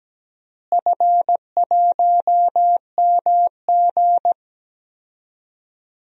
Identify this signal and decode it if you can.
Morse code, heard as a keyed tone, about 17 wpm, 700 Hz. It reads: F1MG
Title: DXCC-100 Call Signs - 4 Characters - Repeated 3x - 17wpm